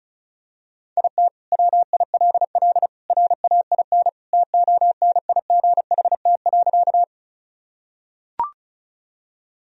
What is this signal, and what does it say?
Morse code, heard as a keyed tone, about 36 wpm, 700 Hz. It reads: IT WILL RAIN TONIGHT.
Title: Sentences from Top 1000 Words - Review Part 1 - 35wpm